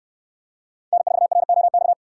Morse code, heard as a keyed tone, about 67 wpm, 700 Hz. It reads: N4ROX